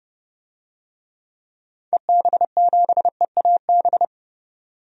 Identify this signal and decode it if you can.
Morse code, heard as a keyed tone, about 31 wpm, 700 Hz. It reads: EB7EAB
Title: DXCC-100 Call Signs - 6 Characters - 30wpm